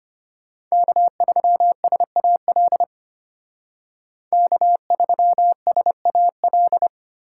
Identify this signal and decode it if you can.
Morse code, heard as a keyed tone, about 33 wpm, 700 Hz. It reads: K3SAL K3SAL